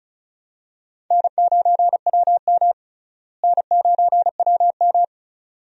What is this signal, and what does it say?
Morse code, heard as a keyed tone, about 36 wpm, 700 Hz. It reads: N9WM N9WM
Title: USA Call Signs - 4 Characters - Repeated 2x - 35wpm